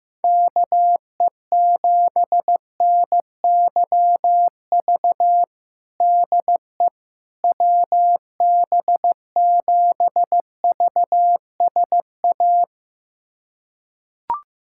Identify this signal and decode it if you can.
Morse code, heard as a keyed tone, about 15 wpm, 700 Hz. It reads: KE7NYV DE WB7VSA
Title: USA Callsign DE Callsign 15wpm